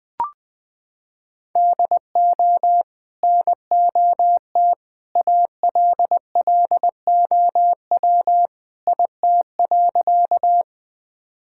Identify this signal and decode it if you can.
Morse code, heard as a keyed tone, about 21 wpm, 700 Hz. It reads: DO NOT ALLOW IT.
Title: Sentences from Top 1000 Words - Review Part 5 - 20wpm